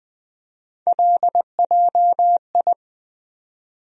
Morse code, heard as a keyed tone, about 21 wpm, 700 Hz. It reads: LJI
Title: USA Call Sign Postfixes - 3 Letters 20wpm